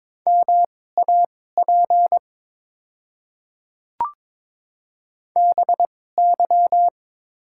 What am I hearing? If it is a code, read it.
Morse code, heard as a keyed tone, about 22 wpm, spaced as for 16 wpm, 700 Hz. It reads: MAP BY
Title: 2x Character Spacing - Top 500 Words - 22wpm